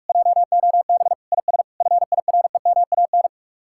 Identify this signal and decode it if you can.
Morse code, heard as a keyed tone, about 49 wpm, 700 Hz. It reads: JOB IS FIREMAN